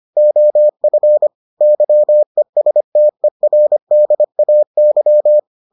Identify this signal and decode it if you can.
Morse code, heard as a keyed tone, about 25 wpm, 600 Hz. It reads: OF YESTERDAY